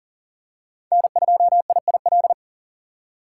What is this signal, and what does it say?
Morse code, heard as a keyed tone, about 40 wpm, 700 Hz. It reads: N2IIL
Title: USA Call Signs - 5 Characters - 2x Character Spacing - 40wpm